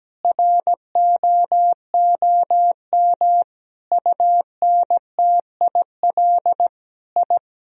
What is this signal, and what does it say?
Morse code, heard as a keyed tone, about 17 wpm, 700 Hz. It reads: ROOM UNTIL I
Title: Sentences from Top 1000 Words - Review Part 1 - 17wpm